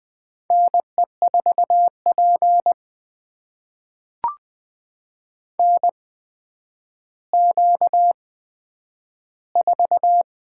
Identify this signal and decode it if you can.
Morse code, heard as a keyed tone, about 20 wpm, spaced as for 10 wpm, 700 Hz. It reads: NE4P NQ4
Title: USA Call Signs - 4 Characters - 8x Character Spacing - 20wpm